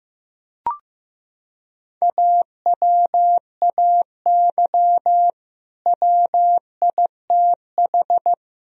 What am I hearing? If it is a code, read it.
Morse code, heard as a keyed tone, about 15 wpm, 700 Hz. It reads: AWAY WITH
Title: Sentences from Top 1000 Words - Review Part 4 - 15wpm